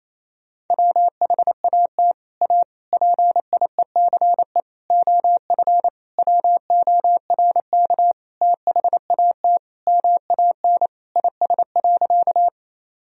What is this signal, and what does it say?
Morse code, heard as a keyed tone, about 28 wpm, 700 Hz. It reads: WHAT A PIECE OF WORK THAT MAN IS.
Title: Sentences from Top 1000 Words - Review Part 2 - 28wpm